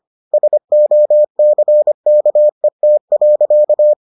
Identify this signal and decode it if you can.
Morse code, heard as a keyed tone, about 25 wpm, 600 Hz. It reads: SOCKET.